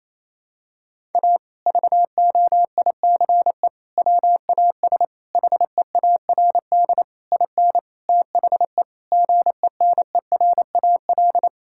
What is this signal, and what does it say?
Morse code, heard as a keyed tone, about 28 wpm, 700 Hz. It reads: A VOICE WAS HEARD IN THE GENERAL
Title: Sentences from Top 1000 Words - Review Part 5 - 28wpm